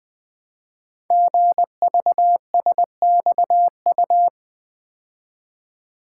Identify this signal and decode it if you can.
Morse code, heard as a keyed tone, about 20 wpm, 700 Hz. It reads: GVSXU
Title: Code Groups - 5 Letters - 20wpm